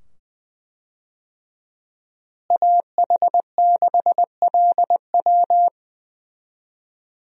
Morse code, heard as a keyed tone, about 20 wpm, 700 Hz. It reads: AH6LW
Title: USA Call Signs - 5 Characters - Repeated 2x - 20wpm